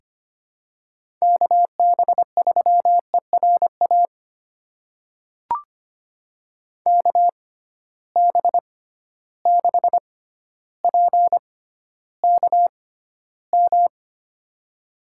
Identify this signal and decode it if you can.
Morse code, heard as a keyed tone, about 25 wpm, spaced as for 13 wpm, 700 Hz. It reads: KB3ERA KB6PKM
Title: USA Call Signs - 6 Characters - 6x Character Spacing - 25wpm